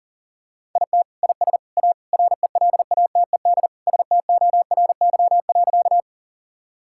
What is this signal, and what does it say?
Morse code, heard as a keyed tone, about 40 wpm, 700 Hz. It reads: IT IS A RELATED STORY.
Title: Sentences from Top 1000 Words - Review Part 5 - 40wpm